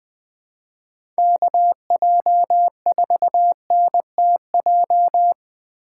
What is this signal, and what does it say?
Morse code, heard as a keyed tone, about 20 wpm, 700 Hz. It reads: KJ4NTJ
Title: USA Call Signs - 6 Characters - 3x Character Spacing - 20wpm